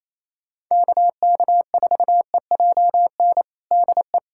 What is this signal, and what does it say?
Morse code, heard as a keyed tone, about 28 wpm, 700 Hz. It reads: KK4EJN DE
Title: USA Callsign DE Callsign 28wpm